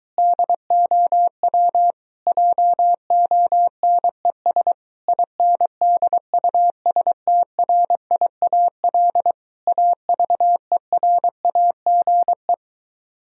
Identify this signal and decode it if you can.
Morse code, heard as a keyed tone, about 23 wpm, 700 Hz. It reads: DOW JONES INDUSTRIAL AVERAGE